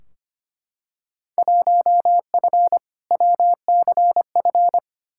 Morse code, heard as a keyed tone, about 25 wpm, 700 Hz. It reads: 1F WCF
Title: Field Day Exchanges - 25wpm